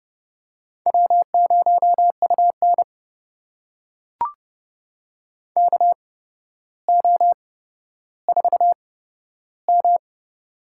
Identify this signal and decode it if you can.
Morse code, heard as a keyed tone, about 31 wpm, spaced as for 13 wpm, 700 Hz. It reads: W0UN KO4M
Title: USA Call Signs - 4 Characters - 8x Character Spacing - 30wpm